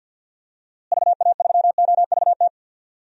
Morse code, heard as a keyed tone, about 48 wpm, 700 Hz. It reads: VA3OVT
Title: DXCC-100 Call Signs - 6 Characters - Repeated 3x - 50wpm